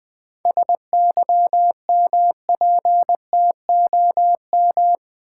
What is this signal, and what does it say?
Morse code, heard as a keyed tone, about 20 wpm, 700 Hz. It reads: SYMPTOM
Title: 7 Letter Words - First 3 Letters Given - 20wpm